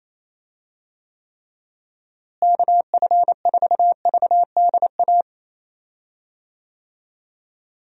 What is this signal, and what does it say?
Morse code, heard as a keyed tone, about 27 wpm, 700 Hz. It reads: KF4VDA
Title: USA Call Signs - 6 Characters - Repeated 2x - 28wpm